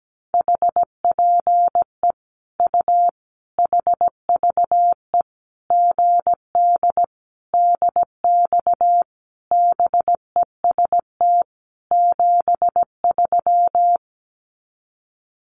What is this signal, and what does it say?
Morse code, heard as a keyed tone, about 17 wpm, 700 Hz. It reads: HPE U HVE GD DX BEST 73
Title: Common QSO Phrases 17wpm